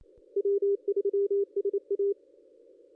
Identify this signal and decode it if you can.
Morse code, heard as a keyed tone, about 28 wpm, 400 Hz. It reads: W3SA
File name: contests / ss-cw-k5tr-op-n5rz -1800